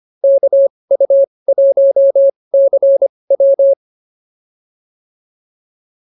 Morse code, heard as a keyed tone, about 25 wpm, spaced as for 20 wpm, 550 Hz. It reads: KU1CW